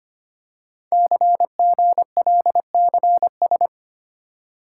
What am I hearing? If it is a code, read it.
Morse code, heard as a keyed tone, about 25 wpm, 700 Hz. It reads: CGLCS
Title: Code Groups - 5 Letters - 25wpm